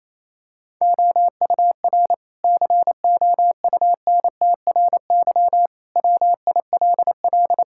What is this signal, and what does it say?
Morse code, heard as a keyed tone, about 28 wpm, 700 Hz. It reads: OUR COUNTRY WILL
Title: Sentences from Top 1000 Words - Review Part 6 - 28wpm